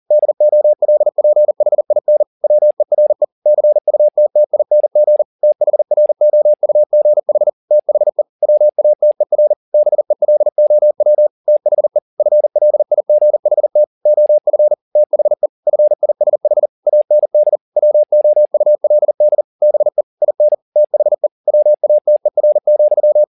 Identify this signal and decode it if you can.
Morse code, heard as a keyed tone, about 40 wpm, 600 Hz. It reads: DOLPHIN WERE CUTTING THROUGH THE WATER BELOW THE FLIGHT OF THE FISH AND WOULD BE IN THE WATER,